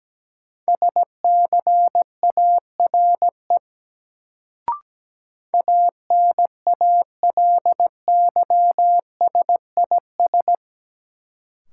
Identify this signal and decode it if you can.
Morse code, heard as a keyed tone, about 17 wpm, 700 Hz. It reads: SCARE ANALYSIS
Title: Sentences from Top 1000 Words - Review Part 5 - New Word Drill - 17wpm